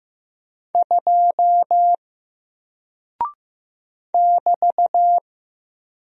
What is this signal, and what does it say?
Morse code, heard as a keyed tone, about 15 wpm, 700 Hz. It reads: 2 <BT>
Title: Single Character (Letter:Number:Punctuation:Prosign) 15wpm